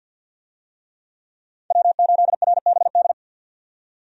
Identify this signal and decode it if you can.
Morse code, heard as a keyed tone, about 52 wpm, 700 Hz. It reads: W8RBD
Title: USA Call Signs - 5 Characters - 50wpm